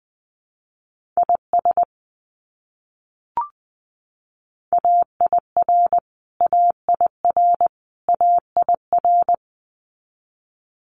Morse code, heard as a keyed tone, about 20 wpm, 700 Hz. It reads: IS AIR AIR AIR
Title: Top 300 Words Repeated 20wpm